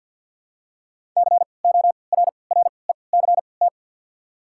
Morse code, heard as a keyed tone, about 53 wpm, spaced as for 29 wpm, 690 Hz. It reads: CORRECT
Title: 3x Character Spacing - Top 500 Words - 50wpm